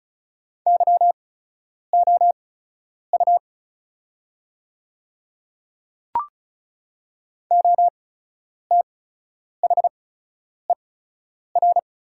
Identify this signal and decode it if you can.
Morse code, heard as a keyed tone, about 36 wpm, spaced as for 10 wpm, 700 Hz. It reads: YOU OTHER
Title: 8x Character Spacing - Top 100 Words - 35wpm